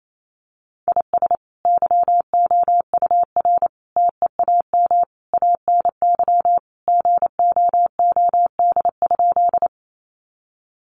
Morse code, heard as a keyed tone, about 28 wpm, 700 Hz. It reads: IS YOUR TEAM ANY GOOD?
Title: Sentences from Top 1500 Words 28wpm